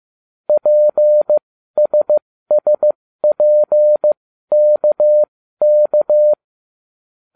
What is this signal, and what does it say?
Morse code, heard as a keyed tone, about 15 wpm, spaced as for 12 wpm, 600 Hz. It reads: PSSPKK